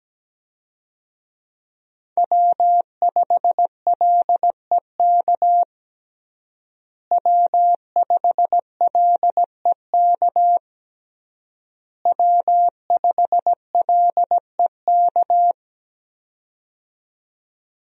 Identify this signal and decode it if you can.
Morse code, heard as a keyed tone, about 17 wpm, 700 Hz. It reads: W5LEK W5LEK W5LEK
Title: USA Call Signs - 5 Characters - Repeated 3x - 17wpm